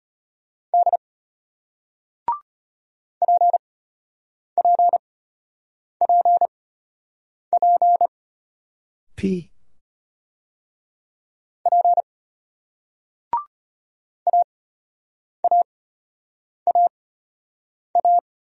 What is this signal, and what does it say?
Morse code, heard as a keyed tone, about 31 wpm, spaced as for 8 wpm, 700 Hz. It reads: D PPPP P AAAA
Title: Speed-Racing Single Letters 25wpm